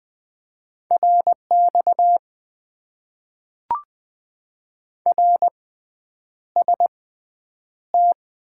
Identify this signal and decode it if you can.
Morse code, heard as a keyed tone, about 21 wpm, spaced as for 9 wpm, 700 Hz. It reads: RX RST
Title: Top 90 QSO Elements - 6x Character Spacing - 20wpm